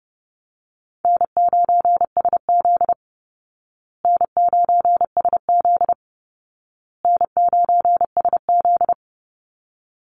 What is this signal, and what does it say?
Morse code, heard as a keyed tone, about 30 wpm, 700 Hz. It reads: N9SZ N9SZ N9SZ
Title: Easy Call Signs Repeated 3x - 30wpm